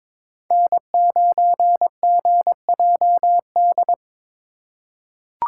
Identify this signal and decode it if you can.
Morse code, heard as a keyed tone, about 22 wpm, 700 Hz. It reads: N9GJD
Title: USA Call Signs - 5 Characters - 3x Character Spacing - 22wpm